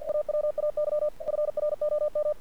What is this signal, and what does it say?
Morse code, heard as a keyed tone, about 49 wpm, 600 Hz. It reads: AWAY FROM
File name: archived / CRSnet